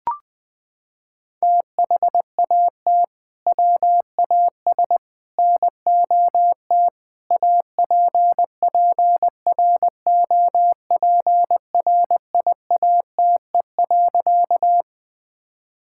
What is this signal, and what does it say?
Morse code, heard as a keyed tone, about 21 wpm, 700 Hz. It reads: THAT WAS NOT APPROPRIATE.
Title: Sentences from Top 1000 Words - Review Part 5 - 20wpm